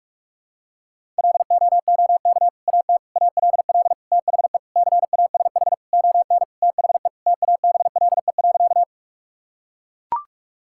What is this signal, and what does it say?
Morse code, heard as a keyed tone, about 41 wpm, 700 Hz. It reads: LOOK AT ALL THE CASH ON THE TABLE.
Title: Sentences from Top 1000 Words - Review Part 5 - 45wpm